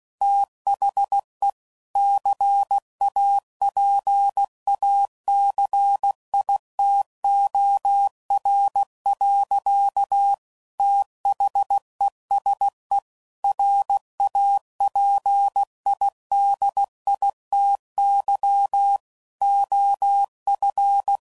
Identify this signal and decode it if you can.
Morse code, heard as a keyed tone, about 16 wpm, 800 Hz. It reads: THE CAPACITOR. THESE RAPIDITY OF
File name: CW Code Groups @15WPM / Technical Subjects / Reactance